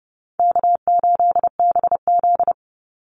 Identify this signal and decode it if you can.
Morse code, heard as a keyed tone, about 30 wpm, 700 Hz. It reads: K8BZ
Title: Easy Call Signs Repeated 3x - 30wpm